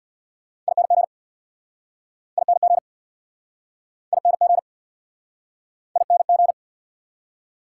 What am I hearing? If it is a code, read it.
Morse code, heard as a keyed tone, about 69 wpm, 690 Hz. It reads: ING ING ING ING